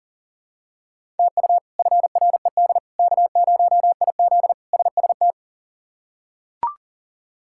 Morse code, heard as a keyed tone, about 40 wpm, 700 Hz. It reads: TU FRED K0IZ SST
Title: K1USN SST QSO Practice - 40wpm